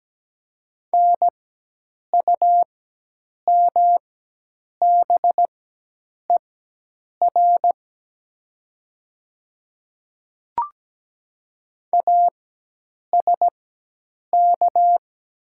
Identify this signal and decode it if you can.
Morse code, heard as a keyed tone, about 17 wpm, spaced as for 8 wpm, 700 Hz. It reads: NUMBER ASK
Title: 4x Character Spacing - Top 300 Words - 17wpm